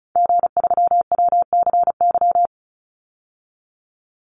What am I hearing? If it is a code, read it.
Morse code, heard as a keyed tone, about 34 wpm, 700 Hz. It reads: G3WCY
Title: Call Signs - 35wpm